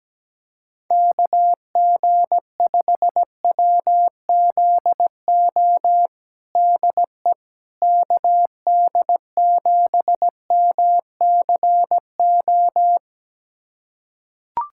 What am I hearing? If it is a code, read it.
Morse code, heard as a keyed tone, about 17 wpm, 700 Hz. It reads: KG5WZO DE KD7MCO
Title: USA Callsign DE Callsign 17wpm